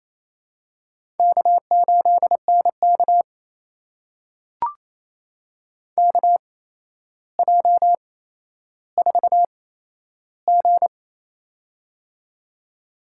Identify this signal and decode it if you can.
Morse code, heard as a keyed tone, about 28 wpm, spaced as for 12 wpm, 700 Hz. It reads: K8NK KJ4G
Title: USA Call Signs - 4 Characters - 8x Character Spacing - 28wpm